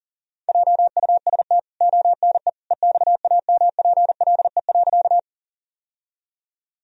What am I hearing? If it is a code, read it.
Morse code, heard as a keyed tone, about 40 wpm, 700 Hz. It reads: JUST ONE EXAMPLE.
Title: Sentences from Top 1000 Words - Review Part 1 - 40wpm